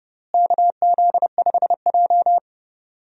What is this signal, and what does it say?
Morse code, heard as a keyed tone, about 30 wpm, 700 Hz. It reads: KZ5J
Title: USA Call Signs - 4 Characters - Repeated 2x - 30wpm